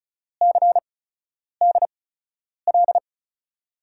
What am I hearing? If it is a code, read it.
Morse code, heard as a keyed tone, about 36 wpm, 700 Hz. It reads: C D L
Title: USA Call Signs - 5 Characters - 8x Character Spacing - 35wpm